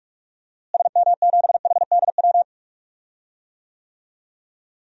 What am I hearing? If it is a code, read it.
Morse code, heard as a keyed tone, about 43 wpm, 700 Hz. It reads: SM7HDW
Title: DXCC-100 Call Signs - 6 Characters - Repeated 2x - 45wpm